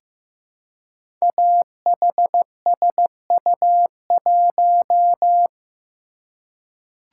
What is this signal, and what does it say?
Morse code, heard as a keyed tone, about 15 wpm, 700 Hz. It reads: AHSU1
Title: Code Groups - 5 Letter:Numbers - 15wpm